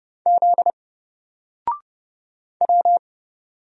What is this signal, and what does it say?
Morse code, heard as a keyed tone, about 30 wpm, 700 Hz. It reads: Z W
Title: Single Letter-Number Alt 30wpm